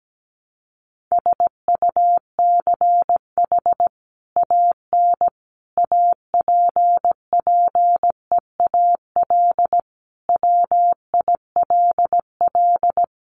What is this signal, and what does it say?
Morse code, heard as a keyed tone, about 17 wpm, 700 Hz. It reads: SUCH AN APPEAL WILL
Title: Sentences from Top 1300 Words 17wpm